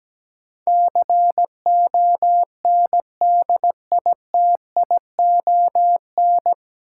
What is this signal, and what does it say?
Morse code, heard as a keyed tone, about 17 wpm, 700 Hz. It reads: CONDITION
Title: Sentences from Top 1000 Words - New Word Drill - 17wpm